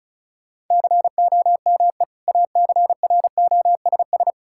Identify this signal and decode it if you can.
Morse code, heard as a keyed tone, about 34 wpm, 700 Hz. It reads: COME ACROSS
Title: Sentences from Top 1000 Words - Review Part 5 - 35wpm